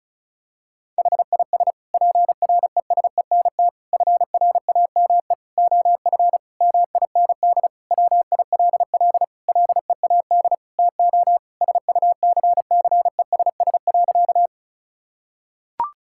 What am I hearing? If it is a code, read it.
Morse code, heard as a keyed tone, about 34 wpm, 700 Hz. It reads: HIS PRESENT FRAME OF MIND WILL LEAD TO SUCCESS.
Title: Sentences from Top 1000 Words - Review Part 6 - 35wpm